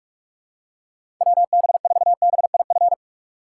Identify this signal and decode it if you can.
Morse code, heard as a keyed tone, about 46 wpm, 700 Hz. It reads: WB4BIF
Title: USA Call Signs - 8x Character Spacing - 45wpm